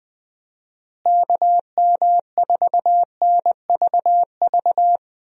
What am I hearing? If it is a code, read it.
Morse code, heard as a keyed tone, about 20 wpm, 700 Hz. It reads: KM4NVV
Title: USA Call Signs - 6 Characters - Repeated 2x - 20wpm